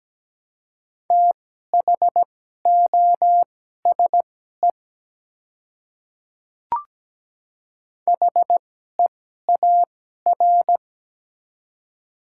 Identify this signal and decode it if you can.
Morse code, heard as a keyed tone, about 17 wpm, spaced as for 12 wpm, 700 Hz. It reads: THOSE HEAR
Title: 2x Character Spacing - Top 300 Words - 17wpm